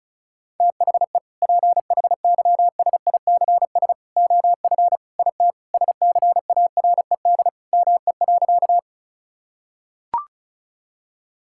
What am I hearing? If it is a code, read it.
Morse code, heard as a keyed tone, about 36 wpm, 700 Hz. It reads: THE PHYSICS OF IT SCARED ME.
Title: Sentences from Top 1000 Words - Review Part 5 - 35wpm